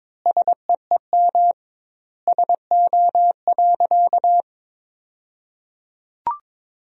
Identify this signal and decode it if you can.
Morse code, heard as a keyed tone, about 23 wpm, 700 Hz. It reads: SEEM SO.
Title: 2x Word-Spacing - Sentences from Top 300 Words - 22wpm